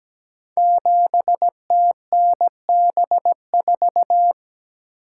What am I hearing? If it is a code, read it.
Morse code, heard as a keyed tone, about 17 wpm, 700 Hz. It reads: 7TNB4
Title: Code Groups - 5 Letter:Numbers - 17wpm